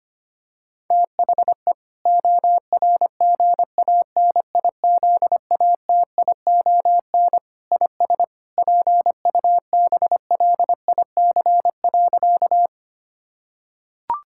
Morse code, heard as a keyed tone, about 25 wpm, 700 Hz. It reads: THE ORGANIZATION IS PUBLIC.
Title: Sentences from Top 1000 Words - Review Part 5 - 25wpm